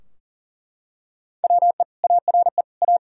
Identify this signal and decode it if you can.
Morse code, heard as a keyed tone, about 40 wpm, 700 Hz. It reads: WE ARE A
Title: Sentences from Top 1000 Words - Review Part 4 - 40wpm